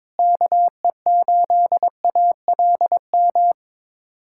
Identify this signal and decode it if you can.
Morse code, heard as a keyed tone, about 22 wpm, 700 Hz. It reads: KE8ALM
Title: USA Callsign DE Callsign 22wpm